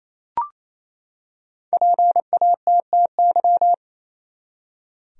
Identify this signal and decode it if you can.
Morse code, heard as a keyed tone, about 28 wpm, 700 Hz. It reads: PATTY
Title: USA Common Names - 28wpm